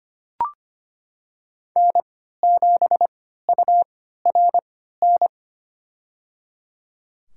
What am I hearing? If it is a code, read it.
Morse code, heard as a keyed tone, about 25 wpm, spaced as for 14 wpm, 700 Hz. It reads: N7URN
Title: USA Call Signs - 5 Characters - 3x Character Spacing - 25wpm